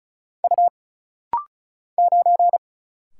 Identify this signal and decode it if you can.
Morse code, heard as a keyed tone, about 34 wpm, 700 Hz. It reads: U 9
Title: Single Letter-Number Alt - Rapid-Fire 35wpm